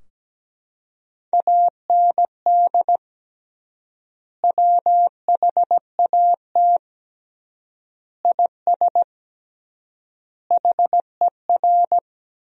Morse code, heard as a keyed tone, about 17 wpm, 700 Hz. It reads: AND WHAT IS HER
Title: 3x Word-Spacing - Sentences from Top 200 Words - 17wpm